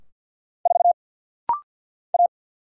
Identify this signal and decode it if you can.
Morse code, heard as a keyed tone, about 48 wpm, 700 Hz. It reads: 4 A
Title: Single Letter-Number - Rapid-Fire 50wpm